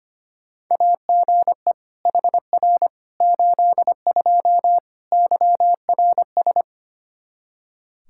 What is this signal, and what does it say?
Morse code, heard as a keyed tone, about 25 wpm, 700 Hz. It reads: AGE HR 82 YRS
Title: K1USN SST QSO Practice 25wpm